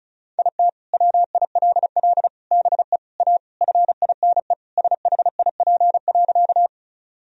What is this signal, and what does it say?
Morse code, heard as a keyed tone, about 34 wpm, 700 Hz. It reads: IT WILL BE A FINE SHIP.
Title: Sentences from Top 1000 Words - Review Part 2 - 35wpm